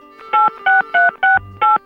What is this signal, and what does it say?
DTMF, heard as telephone touch tones, tones of about 170 ms. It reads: 86368